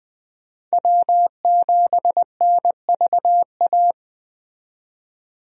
Morse code, heard as a keyed tone, about 20 wpm, 700 Hz. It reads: W7NVA
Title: USA Call Signs - 5 Characters - 20wpm